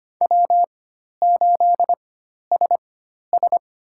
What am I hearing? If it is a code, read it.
Morse code, heard as a keyed tone, about 25 wpm, spaced as for 12 wpm, 700 Hz. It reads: W8SS
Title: USA Call Signs - 4 Characters - 4x Character Spacing - 25wpm